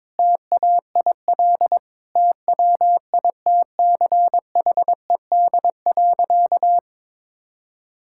Morse code, heard as a keyed tone, about 22 wpm, 700 Hz. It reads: TAIL TWITCHED.
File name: practice / 22wpm / Sentences with 2-Word Phrases - 22wpm